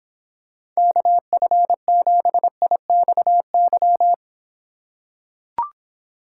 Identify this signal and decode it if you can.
Morse code, heard as a keyed tone, about 26 wpm, 700 Hz. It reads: KF7IXY